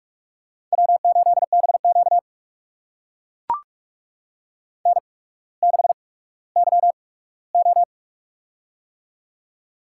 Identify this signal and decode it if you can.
Morse code, heard as a keyed tone, about 44 wpm, spaced as for 19 wpm, 690 Hz. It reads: W8BQ N6YO
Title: USA Call Signs - 4 Characters - 8x Character Spacing - 45wpm